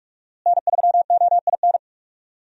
Morse code, heard as a keyed tone, about 42 wpm, 700 Hz. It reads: N3OIN
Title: USA Call Signs - 5 Characters - Repeated 3x - 45wpm